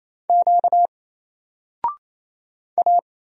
Morse code, heard as a keyed tone, about 28 wpm, 700 Hz. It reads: Q A
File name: Single-Letter-Number-28wpm